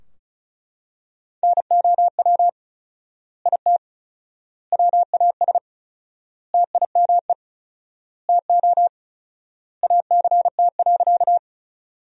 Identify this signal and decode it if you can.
Morse code, heard as a keyed tone, about 36 wpm, 700 Hz. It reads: NOW IT WAS TIME TO ACT.
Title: 4x Word-Spacing - Sentences from Top 200 Words - 35wpm